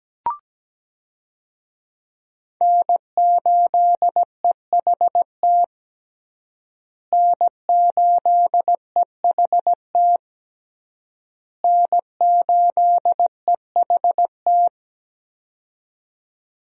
Morse code, heard as a keyed tone, about 17 wpm, 700 Hz. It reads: N8EHT N8EHT N8EHT
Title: USA Call Signs - 5 Characters - Repeated 3x - 17wpm